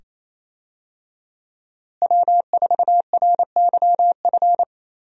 Morse code, heard as a keyed tone, about 28 wpm, 700 Hz. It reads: W4RYF
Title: USA Call Signs - 5 Characters - Repeated 2x - 28wpm